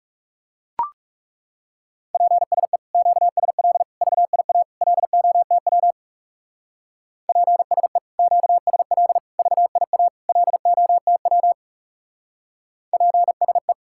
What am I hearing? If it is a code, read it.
Morse code, heard as a keyed tone, about 36 wpm, 700 Hz. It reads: PSE QSL VIA LOTW PSE QSL VIA LOTW PSE
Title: Speed-Racing Common QSO Phrases Encore - 30wpm